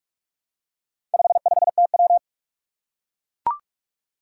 Morse code, heard as a keyed tone, about 44 wpm, 700 Hz. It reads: 55TW
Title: Four Challenging Letter-Number Combinations 45wpm